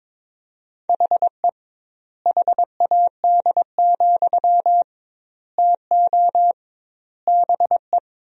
Morse code, heard as a keyed tone, about 22 wpm, 700 Hz. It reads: HE HAD, TO BE
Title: 2x Word-Spacing - Sentences from Top 500 Words - 22wpm